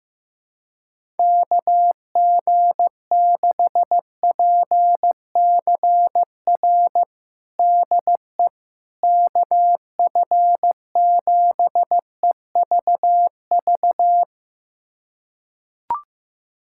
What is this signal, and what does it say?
Morse code, heard as a keyed tone, about 15 wpm, 700 Hz. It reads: KG6PCR DE KF7EVV
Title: USA Callsign DE Callsign - 6 Characters - 15wpm